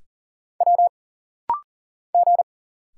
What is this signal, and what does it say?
Morse code, heard as a keyed tone, about 40 wpm, 700 Hz. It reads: W G
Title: Single Letters - Rapid-Fire 40wpm